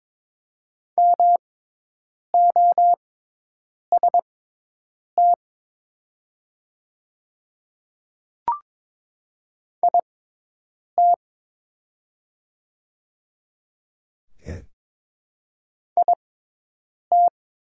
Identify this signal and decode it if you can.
Morse code, heard as a keyed tone, about 22 wpm, spaced as for 8 wpm, 700 Hz. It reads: MOST IT IT
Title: 6x Character Spacing - Top 100 Words - 22wpm